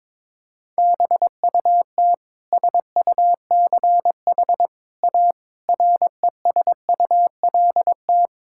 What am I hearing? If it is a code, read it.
Morse code, heard as a keyed tone, about 22 wpm, 700 Hz. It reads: BUT SUCH A RESULT
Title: Sentences from Top 1000 Words - Review Part 5 - 22wpm